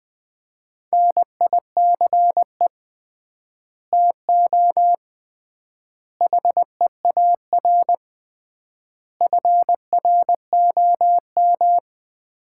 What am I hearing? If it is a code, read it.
Morse code, heard as a keyed tone, about 21 wpm, 700 Hz. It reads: NICE TO HEAR FROM
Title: Common QSO Phrases - 3x Word-Spacing - 20wpm